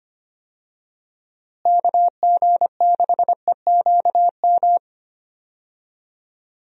Morse code, heard as a keyed tone, about 25 wpm, 700 Hz. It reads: KG6EQM
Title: USA Call Signs - 6 Characters - Repeated 3x - 25wpm